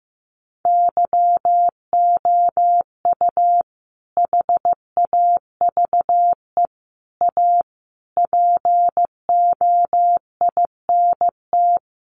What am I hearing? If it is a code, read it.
Morse code, heard as a keyed tone, about 15 wpm, 700 Hz. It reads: YOU HAVE A POINT
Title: Sentences from Top 1400 Words 15wpm